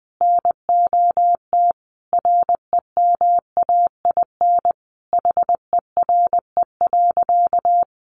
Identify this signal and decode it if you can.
Morse code, heard as a keyed tone, about 20 wpm, 700 Hz. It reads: NOT REMAIN HERE.
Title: Sentences from Top 1200 Words 20wpm